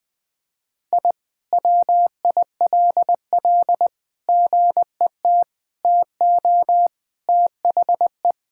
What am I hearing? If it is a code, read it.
Morse code, heard as a keyed tone, about 20 wpm, 700 Hz. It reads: I WILL GET TO THE
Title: Sentences from Top 1000 Words - Review Part 6 - 20wpm